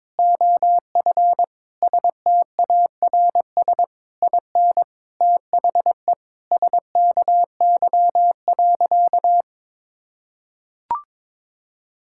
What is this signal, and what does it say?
Morse code, heard as a keyed tone, about 22 wpm, 700 Hz. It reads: OF STARS IN THE SKY.
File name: practice / 22wpm / Sentences with 4-Word Phrases - 22wpm